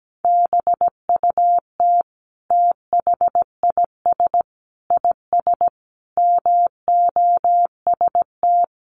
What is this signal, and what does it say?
Morse code, heard as a keyed tone, about 17 wpm, 700 Hz. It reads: BUT THIS IS MOST
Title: Sentences from Top 1400 Words 17wpm